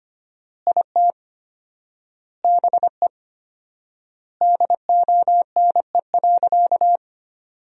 Morse code, heard as a keyed tone, about 25 wpm, 700 Hz. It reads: IT BE DONE.
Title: 4x Word-Spacing - Sentences from Top 500 Words - 25wpm